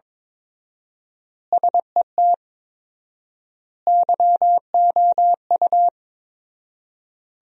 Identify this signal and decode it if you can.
Morse code, heard as a keyed tone, about 22 wpm, 700 Hz. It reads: SET YOU
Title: 4x Word-Spacing - Sentences from Top 200 Words - 22wpm